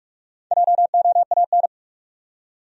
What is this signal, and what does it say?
Morse code, heard as a keyed tone, about 43 wpm, 700 Hz. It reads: JOAN
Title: USA Common Names - Max 4 Letters - 45wpm